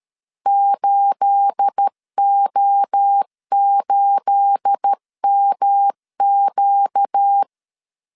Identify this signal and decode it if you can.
Morse code, heard as a keyed tone, about 13 wpm, 800 Hz. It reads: 8O8MQ
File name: CW / DET / 1 9 1500C S A 12wpm 3800